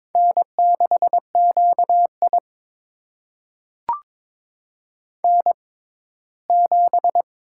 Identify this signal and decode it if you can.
Morse code, heard as a keyed tone, about 22 wpm, spaced as for 15 wpm, 700 Hz. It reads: N6QI N7
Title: USA Call Signs - 6x Character Spacing - 22wpm